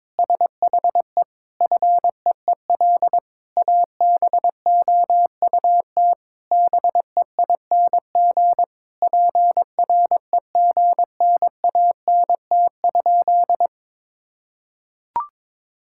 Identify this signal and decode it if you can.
Morse code, heard as a keyed tone, about 22 wpm, 700 Hz. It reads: SHE FEEL ABOUT BEING PREGNANT?
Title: Sentences from Top 1000 Words - Review Part 6 - 22wpm